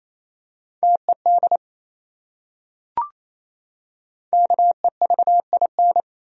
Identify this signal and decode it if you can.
Morse code, heard as a keyed tone, about 28 wpm, 700 Hz. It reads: TED KEVIN
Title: USA Common Names - 28wpm